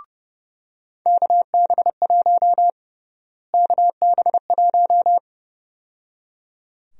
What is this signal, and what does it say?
Morse code, heard as a keyed tone, about 30 wpm, 700 Hz. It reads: KB1 KB1
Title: USA Call Sign Prefixes - Repeated 2x - 30wpm